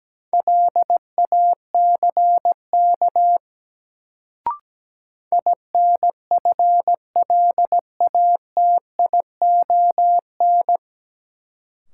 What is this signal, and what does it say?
Morse code, heard as a keyed tone, about 17 wpm, 700 Hz. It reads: LACK INFLATION
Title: Sentences from Top 1100 Words - New Word Drill - 17wpm